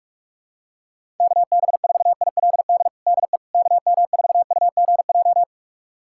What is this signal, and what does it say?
Morse code, heard as a keyed tone, about 46 wpm, 700 Hz. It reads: KB4ILD DE KM4UGJ
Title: USA Callsign DE Callsign - 6 Characters - 45wpm